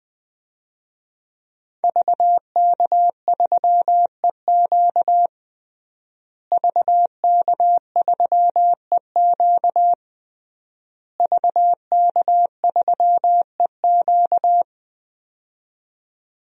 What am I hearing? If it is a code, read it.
Morse code, heard as a keyed tone, about 20 wpm, 700 Hz. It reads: VK3EQ VK3EQ VK3EQ
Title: DXCC-100 Call Signs - 5 Characters - Repeated 3x - 20wpm